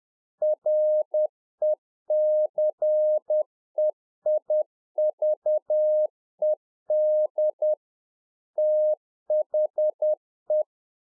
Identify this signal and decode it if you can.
Morse code, heard as a keyed tone, about 10 wpm, 610 Hz. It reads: RECEIVED THE
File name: UNFOGETTABLE QSO 10WPM